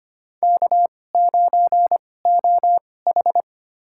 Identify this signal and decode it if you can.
Morse code, heard as a keyed tone, about 25 wpm, spaced as for 18 wpm, 700 Hz. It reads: K9OH
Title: USA Call Signs - 2x Character Spacing - 25wpm